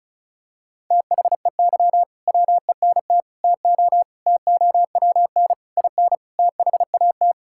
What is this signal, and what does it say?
Morse code, heard as a keyed tone, about 36 wpm, 690 Hz. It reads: THEY WENT TO TOWN IN THAT
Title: Sentences from Top 1000 Words - Review Part 5 - 35wpm